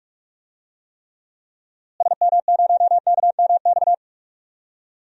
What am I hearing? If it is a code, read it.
Morse code, heard as a keyed tone, about 46 wpm, 700 Hz. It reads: SM0KMX
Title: DXCC-100 Call Signs - 6 Characters - Repeated 2x - 45wpm